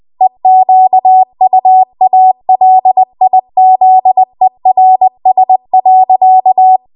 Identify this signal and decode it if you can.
Morse code, heard as a keyed tone, about 20 wpm, 750 Hz. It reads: EQUALIZERS.